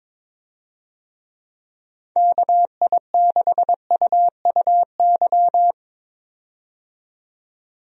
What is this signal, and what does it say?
Morse code, heard as a keyed tone, about 22 wpm, 700 Hz. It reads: KI6UUY
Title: USA Call Signs - 6 Characters - 22wpm